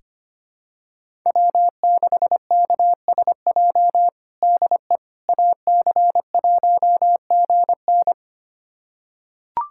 Morse code, heard as a keyed tone, about 25 wpm, 700 Hz. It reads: W6KSJ DE AC1GN
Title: USA Callsign DE Callsign - 5 Characters - 25wpm